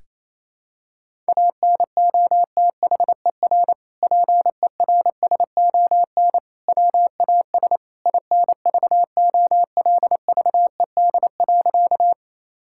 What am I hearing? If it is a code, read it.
Morse code, heard as a keyed tone, about 27 wpm, 700 Hz. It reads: ANOTHER PERSON WAS INVOLVED.
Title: Sentences from Top 1000 Words - Review Part 5 - 28wpm